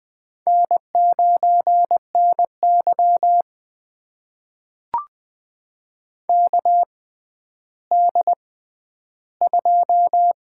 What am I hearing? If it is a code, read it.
Morse code, heard as a keyed tone, about 20 wpm, spaced as for 11 wpm, 700 Hz. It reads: N9NY KD2
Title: USA Call Signs - 4 Characters - 6x Character Spacing - 20wpm